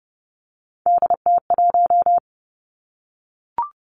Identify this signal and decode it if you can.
Morse code, heard as a keyed tone, about 30 wpm, 700 Hz. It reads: DT1
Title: Three Challenging Letter-Number Combinations 30wpm